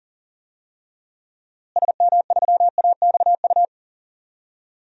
Morse code, heard as a keyed tone, about 40 wpm, 700 Hz. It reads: SM3AXU
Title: DXCC-100 Call Signs - 6 Characters - Repeated 3x - 40wpm